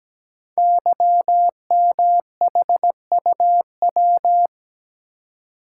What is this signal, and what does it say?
Morse code, heard as a keyed tone, about 17 wpm, 700 Hz. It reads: YMHUW
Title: Code Groups - 5 Letter:Numbers - 17wpm